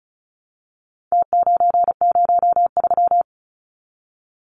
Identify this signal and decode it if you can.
Morse code, heard as a keyed tone, about 34 wpm, 700 Hz. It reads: T903
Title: Numbers - Contest Style - 4-Digit 35wpm